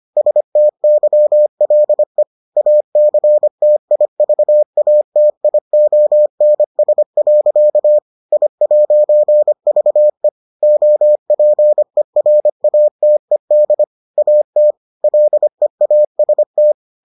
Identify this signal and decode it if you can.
Morse code, heard as a keyed tone, about 25 wpm, 600 Hz. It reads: STYLE ACTIVATIONS. I'VE OPERATED AT LEAST